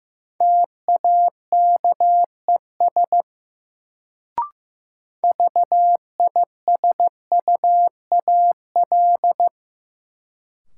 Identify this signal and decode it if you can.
Morse code, heard as a keyed tone, about 15 wpm, 700 Hz. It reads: TAKES VISUAL
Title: Sets of 5 Words Encore - New Word Drill - 15wpm